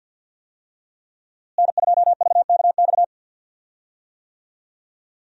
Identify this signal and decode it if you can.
Morse code, heard as a keyed tone, about 52 wpm, 700 Hz. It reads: N2VKX
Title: USA Call Signs - 5 Characters - Repeated 2x - 50wpm